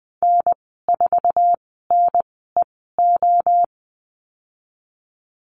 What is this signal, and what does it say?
Morse code, heard as a keyed tone, about 20 wpm, spaced as for 15 wpm, 700 Hz. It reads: N4NEO
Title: 048 Lesson 9g - Number 4 - Call Signs 20wpm